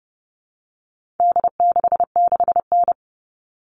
Morse code, heard as a keyed tone, about 30 wpm, 700 Hz. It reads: D66N
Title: All Cut Numbers - 4-Digit 30wpm